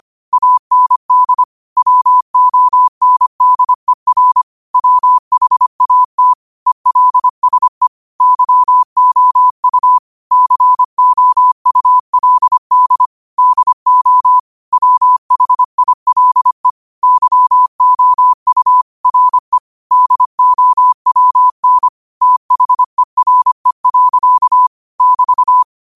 Morse code, heard as a keyed tone, about 25 wpm, 1000 Hz. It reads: AND WONDER WHAT ELSE YOU COULD DO WHILE YOU RE DOWN THERE. <BT>
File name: qotdfiles / 25wpm